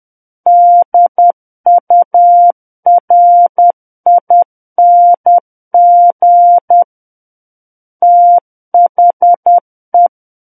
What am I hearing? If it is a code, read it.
Morse code, heard as a keyed tone, about 10 wpm, 700 Hz. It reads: DURING THE
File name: Book One / 01-Chapter 1 - 10 WPM